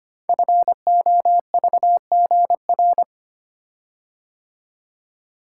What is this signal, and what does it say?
Morse code, heard as a keyed tone, about 25 wpm, 700 Hz. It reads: FOVGR
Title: Code Groups - 5 Letters - 25wpm